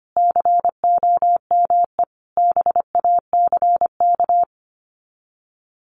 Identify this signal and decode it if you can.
Morse code, heard as a keyed tone, about 25 wpm, 700 Hz. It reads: COME BACK
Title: Sets of 5 Words 25wpm